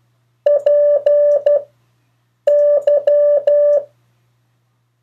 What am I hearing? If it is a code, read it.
Morse code, heard as a keyed tone, about 12 wpm, 580 Hz. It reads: P Y
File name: 3of4-words-2